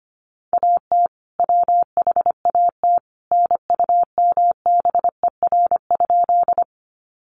Morse code, heard as a keyed tone, about 25 wpm, 700 Hz. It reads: AT WHAT NUMBER?
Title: Sentences from Top 100 Words 25wpm